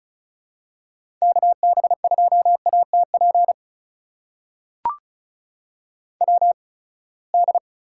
Morse code, heard as a keyed tone, about 36 wpm, spaced as for 24 wpm, 700 Hz. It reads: KB2ATP WD
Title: USA Call Signs - 6 Characters - 8x Character Spacing - 35wpm